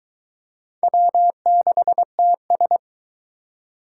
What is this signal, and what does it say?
Morse code, heard as a keyed tone, about 21 wpm, 700 Hz. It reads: W6TS